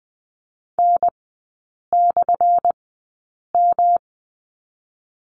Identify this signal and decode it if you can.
Morse code, heard as a keyed tone, about 20 wpm, spaced as for 8 wpm, 700 Hz. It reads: N/M
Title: 178 Lesson 29e - Slash - Three Characters 20wpm